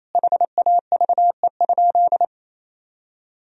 Morse code, heard as a keyed tone, about 28 wpm, 700 Hz. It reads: HAVE?
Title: Sentences from Top 1000 Words - Review Part 3 - 28wpm